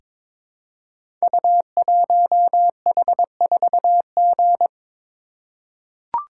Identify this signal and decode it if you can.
Morse code, heard as a keyed tone, about 22 wpm, 700 Hz. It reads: U1H4G
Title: Code Groups - 5 Letter:Numbers - 22wpm